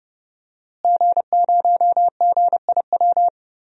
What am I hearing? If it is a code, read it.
Morse code, heard as a keyed tone, about 29 wpm, 700 Hz. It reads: G0GIW
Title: DXCC-100 Call Signs - 5 Characters - Repeated 3x - 30wpm